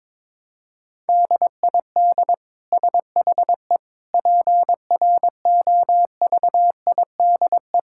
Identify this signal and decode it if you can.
Morse code, heard as a keyed tone, about 22 wpm, 700 Hz. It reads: DID SHE PROVIDE
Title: Sentences from Top 1000 Words - Review Part 5 - 22wpm